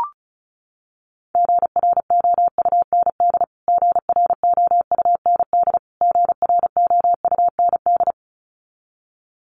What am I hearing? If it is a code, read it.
Morse code, heard as a keyed tone, about 34 wpm, 700 Hz. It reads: GROUND GROUND GROUND
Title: Top 500 Words Repeated 35wpm